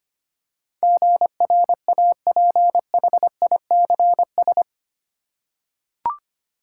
Morse code, heard as a keyed tone, about 25 wpm, 700 Hz. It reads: GRAPHICS
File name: Eight Letter Words - 25wpm